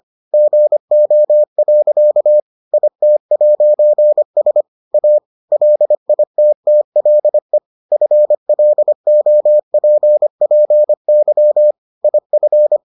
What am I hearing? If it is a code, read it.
Morse code, heard as a keyed tone, about 25 wpm, 600 Hz. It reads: GO. IT'S A LITTLE FLOPPY IF